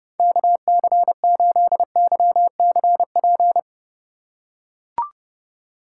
Morse code, heard as a keyed tone, about 30 wpm, 700 Hz. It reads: KC8YCP
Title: USA Call Signs - 4x Character Spacing - 30wpm